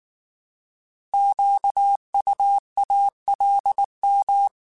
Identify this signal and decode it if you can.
Morse code, heard as a keyed tone, about 19 wpm, 800 Hz. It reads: QUALM
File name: Alphabetical / Alphabetical mp3s / Qs @18